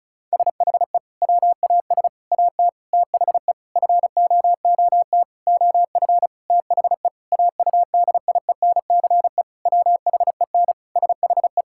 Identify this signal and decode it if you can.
Morse code, heard as a keyed tone, about 34 wpm, 700 Hz. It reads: SHE WAS AT THE FOOT OF THE AUDIENCE WHEN SHE
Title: Sentences from Top 1000 Words - Review Part 5 - 35wpm